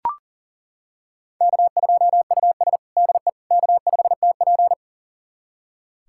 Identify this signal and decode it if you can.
Morse code, heard as a keyed tone, about 40 wpm, 700 Hz. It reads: K2US DE K5TP
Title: USA Callsign DE Callsign - 4 Characters - 40wpm